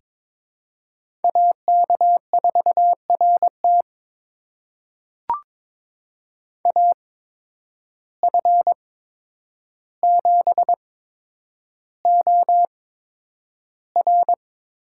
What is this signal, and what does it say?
Morse code, heard as a keyed tone, about 22 wpm, spaced as for 9 wpm, 700 Hz. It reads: AK4RT AF7OR
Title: USA Call Signs - 8x Character Spacing - 22wpm